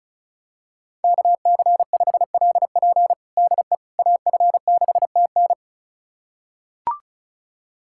Morse code, heard as a keyed tone, about 36 wpm, 700 Hz. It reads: KC5LP DE AF6TN
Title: USA Callsign DE Callsign - 5 Characters - 35wpm